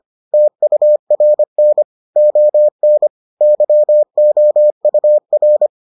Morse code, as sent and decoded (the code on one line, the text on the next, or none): - ..- .-. -. / --- -. / -.-- --- ..- .-.
TURN ON YOUR